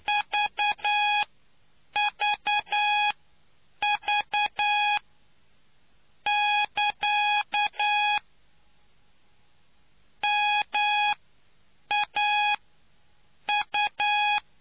Morse code, heard as a keyed tone, about 9 wpm, spaced as for 7 wpm, 830 Hz. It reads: VVV <KA> MAU